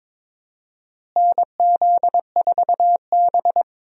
Morse code, heard as a keyed tone, about 22 wpm, 700 Hz. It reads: NZ4B
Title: USA Call Signs - 4 Characters - Repeated 3x - 22wpm